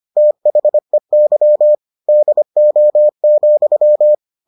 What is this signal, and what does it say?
Morse code, heard as a keyed tone, about 25 wpm, 600 Hz. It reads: THEY DO,